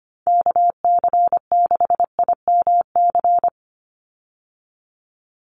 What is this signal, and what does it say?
Morse code, heard as a keyed tone, about 25 wpm, 690 Hz. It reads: KC6IMC
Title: USA Call Signs 25wpm